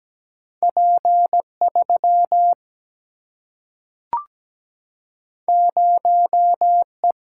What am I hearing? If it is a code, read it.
Morse code, heard as a keyed tone, about 17 wpm, 700 Hz. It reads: P3 0E
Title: Single-Letter Single-Number 17wpm